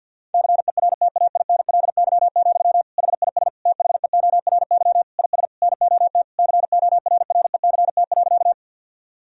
Morse code, heard as a keyed tone, about 50 wpm, 700 Hz. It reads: CERTAINLY, HIS THEORY IS NOT CORRECT.
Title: Sentences from Top 1000 Words - Review Part 3 - 50wpm